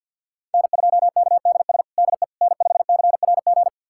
Morse code, heard as a keyed tone, about 46 wpm, 700 Hz. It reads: N2KDS DE N5CRG
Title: USA Callsign DE Callsign - 5 Characters - 50wpm